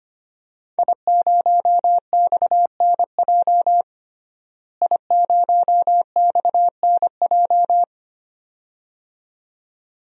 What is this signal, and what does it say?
Morse code, heard as a keyed tone, about 25 wpm, 700 Hz. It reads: I0XNJ I0XNJ
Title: DXCC-100 Call Signs - 5 Characters - Repeated 2x - 25wpm